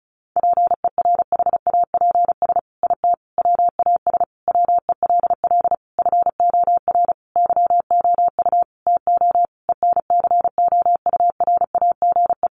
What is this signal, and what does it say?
Morse code, heard as a keyed tone, about 34 wpm, 700 Hz. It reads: PERHAPS IT WAS WELL FOR YOU TO ENCOURAGE
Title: Sentences from Top 1400 Words 35wpm